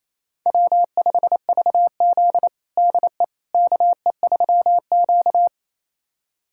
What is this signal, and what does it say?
Morse code, heard as a keyed tone, about 28 wpm, 700 Hz. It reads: W5VZ DE KE3Q
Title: USA Callsign DE Callsign - 4 Characters - 28wpm